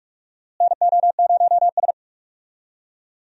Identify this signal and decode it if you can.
Morse code, heard as a keyed tone, about 42 wpm, 700 Hz. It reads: NO0S
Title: USA Call Signs - 4 Characters - Repeated 3x - 45wpm